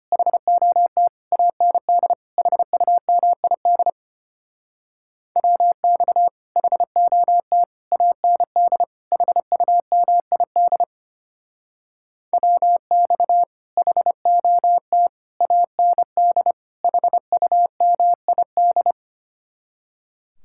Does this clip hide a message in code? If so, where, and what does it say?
Yes: Morse code from 0.1 to 18.9 s, heard as a keyed tone, about 32 wpm, 700 Hz. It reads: HOT AND HUMID WX HOT AND HUMID WX HOT AND HUMID